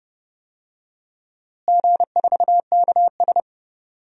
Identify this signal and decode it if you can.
Morse code, heard as a keyed tone, about 30 wpm, 700 Hz. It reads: G4KS
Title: DXCC-100 Call Signs - 4 Characters - Repeated 2x - 30wpm